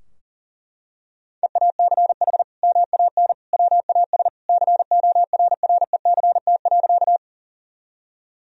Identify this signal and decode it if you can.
Morse code, heard as a keyed tone, about 40 wpm, 700 Hz. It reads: EACH MAN WAS CORRECT.
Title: Sentences from Top 1000 Words - Review Part 2 - 40wpm